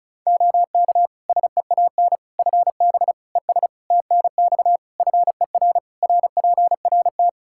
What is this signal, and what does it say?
Morse code, heard as a keyed tone, about 36 wpm, 700 Hz. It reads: OK SEAN FB ES TNX FER RPRT
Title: Practice QSO - Exchange 2 - Info - 35wpm